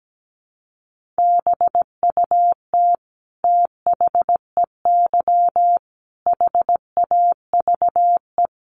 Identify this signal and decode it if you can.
Morse code, heard as a keyed tone, about 17 wpm, 700 Hz. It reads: BUT THEY HAVE
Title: Sentences from Top 1200 Words 17wpm